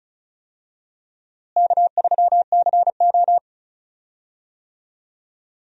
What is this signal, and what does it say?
Morse code, heard as a keyed tone, about 36 wpm, 700 Hz. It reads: K3CO
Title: USA Call Signs - 4 Characters - Repeated 2x - 35wpm